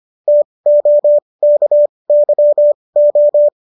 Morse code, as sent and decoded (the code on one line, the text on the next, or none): - --- -.- -.-- ---
TOKYO